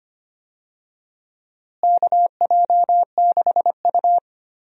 Morse code, heard as a keyed tone, about 25 wpm, 700 Hz. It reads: KJ6U
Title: USA Call Signs - 4 Characters - Repeated 2x - 25wpm